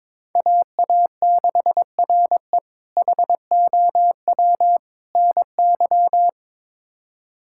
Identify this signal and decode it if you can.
Morse code, heard as a keyed tone, about 21 wpm, 700 Hz. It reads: AA6RE HOW NY
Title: K1USN SST QSO Practice - 22wpm